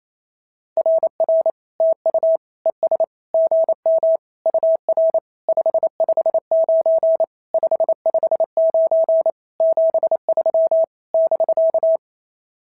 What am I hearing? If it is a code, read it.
Morse code, heard as a keyed tone, about 28 wpm, 650 Hz. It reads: RR TU ES GM UR 559 559 73 <BK>
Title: SOTA QSO Practice 28wpm